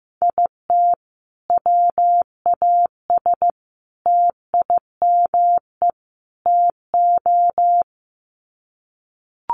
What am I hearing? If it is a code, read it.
Morse code, heard as a keyed tone, about 15 wpm, 700 Hz. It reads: IT WAS TIME TO
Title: Sets of 4 Words 15wpm